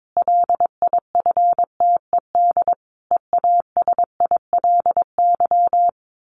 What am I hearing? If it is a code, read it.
Morse code, heard as a keyed tone, about 22 wpm, 700 Hz. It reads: LIFTED EASILY
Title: A Sound Of Thunder 22wpm